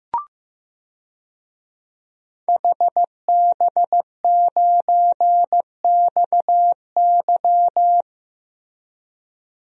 Morse code, heard as a keyed tone, about 15 wpm, 700 Hz. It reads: HB9XY
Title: DXCC-100 Call Signs - 5 Characters - Repeated 3x - 15wpm